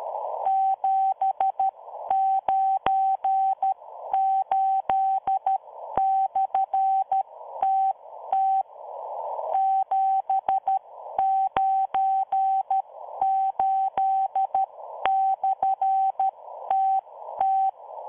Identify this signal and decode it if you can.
Morse code, heard as a keyed tone, about 13 wpm, spaced as for 11 wpm, 770 Hz, with radio noise. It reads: 798/TT 798/TT